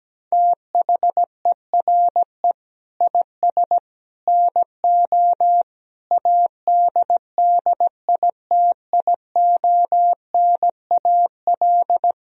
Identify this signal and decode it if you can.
Morse code, heard as a keyed tone, about 17 wpm, 700 Hz. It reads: THERE IS NO ADDITIONAL
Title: Sentences from Top 1000 Words - Review Part 4 - 17wpm